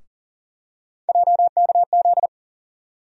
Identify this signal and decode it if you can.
Morse code, heard as a keyed tone, about 40 wpm, 700 Hz. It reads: JKZ
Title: USA Call Sign Postfixes - 3 Letters 40wpm